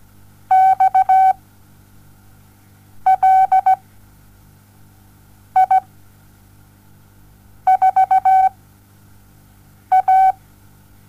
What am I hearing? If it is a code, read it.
Morse code, heard as a keyed tone, about 17 wpm, spaced as for 5 wpm, 760 Hz. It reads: XLI4A
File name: Onlinelehrgang / morsen / DJ4UF Morsen07